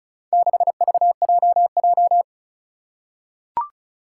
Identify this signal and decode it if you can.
Morse code, heard as a keyed tone, about 34 wpm, 700 Hz. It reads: 6VJJ
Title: Four Challenging Letter-Number Combinations 35wpm